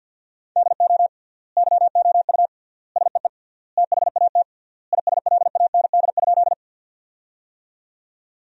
Morse code, heard as a keyed tone, about 52 wpm, 700 Hz. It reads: DO YOU SEE THAT ISLAND?
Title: 3x Word-Spacing - Sentences from Top 500 Words - 50wpm